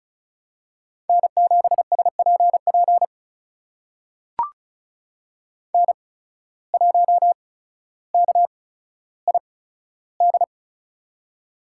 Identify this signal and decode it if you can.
Morse code, heard as a keyed tone, about 36 wpm, spaced as for 15 wpm, 700 Hz. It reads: N7SPP N1KID
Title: USA Call Signs - 5 Characters - 8x Character Spacing - 35wpm